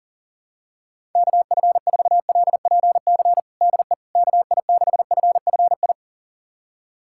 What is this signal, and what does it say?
Morse code, heard as a keyed tone, about 40 wpm, 700 Hz. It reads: KF4LPC DE KI6FFI
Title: USA Callsign DE Callsign - 6 Characters - 40wpm